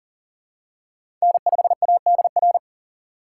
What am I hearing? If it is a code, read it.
Morse code, heard as a keyed tone, about 40 wpm, 700 Hz. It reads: N5ADR
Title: USA Call Signs - 5 Characters - 2x Character Spacing - 40wpm